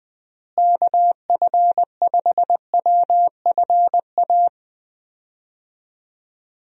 Morse code, heard as a keyed tone, about 20 wpm, 700 Hz. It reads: KF5WFA
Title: USA Call Signs - 6 Characters - Repeated 2x - 20wpm